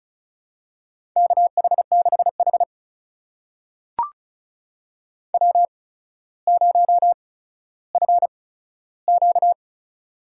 Morse code, heard as a keyed tone, about 34 wpm, spaced as for 15 wpm, 700 Hz. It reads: KH6H W0FQ
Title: USA Call Signs - 4 Characters - 8x Character Spacing - 35wpm